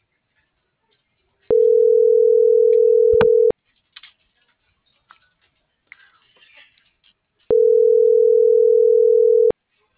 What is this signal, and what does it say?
Ringback tone (North America), 440 + 480 Hz.